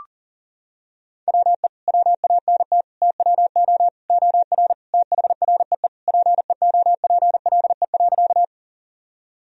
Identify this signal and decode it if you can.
Morse code, heard as a keyed tone, about 37 wpm, 700 Hz. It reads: WE WANT TWO OR THREE PEOPLE.
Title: Sentences from Top 1000 Words - Review Part 1 - 40wpm